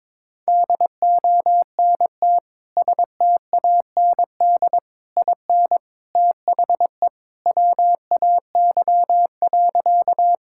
Morse code, heard as a keyed tone, about 21 wpm, 700 Hz. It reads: DONT STAND IN THE WAY.
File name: practice / 22wpm / Sentences with 3-Word Phrases - 22wpm